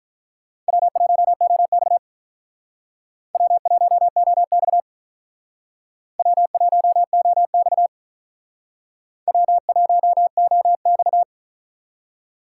Morse code, heard as a keyed tone, about 45 wpm, 700 Hz. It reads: W1OX W1OX W1OX W1OX